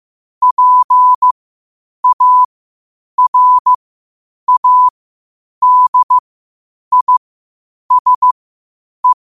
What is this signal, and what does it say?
Morse code, heard as a keyed tone, about 15 wpm, spaced as for 8 wpm, 1000 Hz. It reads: PARADISE
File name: qotdfiles / 8wpm